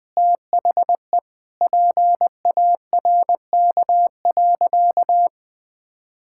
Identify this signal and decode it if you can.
Morse code, heard as a keyed tone, about 20 wpm, 700 Hz. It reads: THE PARK.
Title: Sentences with 5-Word Phrases - Spoken Beginning - 20wpm